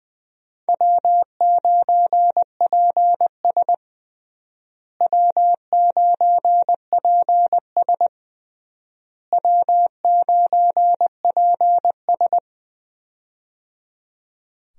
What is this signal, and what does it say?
Morse code, heard as a keyed tone, about 20 wpm, 700 Hz. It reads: W9PS W9PS W9PS
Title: USA Call Signs - 4 Characters - Repeated 3x - 20wpm